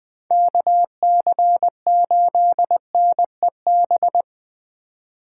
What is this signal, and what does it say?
Morse code, heard as a keyed tone, about 20 wpm, 700 Hz. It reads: KC8NEB
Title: Speed Racing - USA 2x3 Call Signs 17wpm